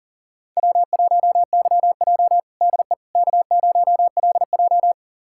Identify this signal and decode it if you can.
Morse code, heard as a keyed tone, about 40 wpm, 700 Hz. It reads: W1YJ DE K0LJ
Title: USA Callsign DE Callsign - 4 Characters - 40wpm